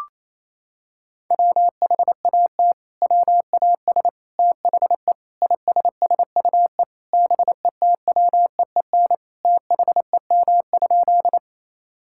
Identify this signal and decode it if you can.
Morse code, heard as a keyed tone, about 28 wpm, 690 Hz. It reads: WHAT WAS THE ISSUE BETWEEN THEM?
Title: Sentences from Top 1000 Words - Review Part 6 - 28wpm